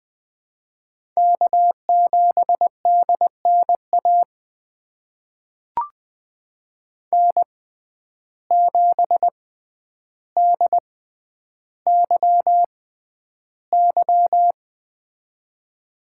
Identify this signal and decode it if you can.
Morse code, heard as a keyed tone, about 20 wpm, spaced as for 10 wpm, 700 Hz. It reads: K7DNA N7DYY
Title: USA Call Signs - 5 Characters - 6x Character Spacing - 20wpm